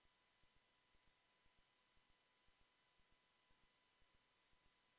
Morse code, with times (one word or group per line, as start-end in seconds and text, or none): none